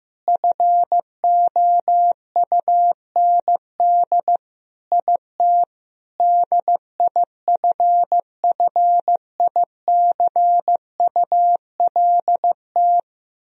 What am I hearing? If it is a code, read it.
Morse code, heard as a keyed tone, about 16 wpm, 700 Hz. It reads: FOUND IT DIFFICULT
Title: Sentences from Top 1000 Words - Review Part 4 - 15wpm